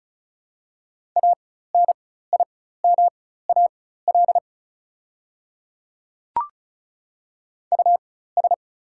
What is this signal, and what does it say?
Morse code, heard as a keyed tone, about 36 wpm, spaced as for 16 wpm, 690 Hz. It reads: ANIMAL US
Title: 4x Character Spacing - Top 300 Words - 35wpm